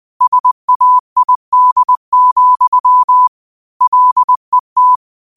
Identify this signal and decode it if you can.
Morse code, heard as a keyed tone, about 20 wpm, 1000 Hz. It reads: SAID, LET
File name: 20wpm